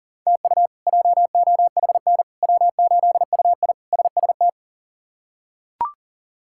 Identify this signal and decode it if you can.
Morse code, heard as a keyed tone, about 40 wpm, 700 Hz. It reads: TU JOHN W8UI SST
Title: K1USN SST QSO Practice - 40wpm